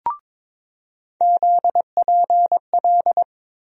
Morse code, heard as a keyed tone, about 23 wpm, 700 Hz. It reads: ZPL
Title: USA Call Sign Postfixes - 3 Letters 22wpm